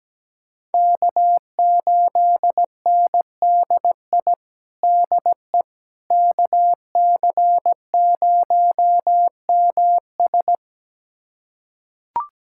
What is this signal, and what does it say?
Morse code, heard as a keyed tone, about 17 wpm, 700 Hz. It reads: K8NDI DE KC0MS
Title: USA Callsign DE Callsign - 5 Characters - 17wpm